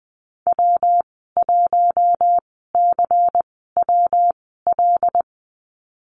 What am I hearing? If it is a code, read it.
Morse code, heard as a keyed tone, about 20 wpm, spaced as for 15 wpm, 700 Hz. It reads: W1CWL
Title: 161 Lesson 26g - Letter g - Call Signs 20wpm